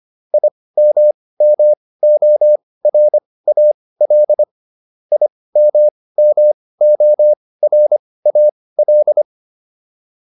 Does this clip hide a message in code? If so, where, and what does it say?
Yes: Morse code from 0.3 to 9.2 s, heard as a keyed tone, about 25 wpm, spaced as for 18 wpm, 600 Hz. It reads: IMMORAL IMMORAL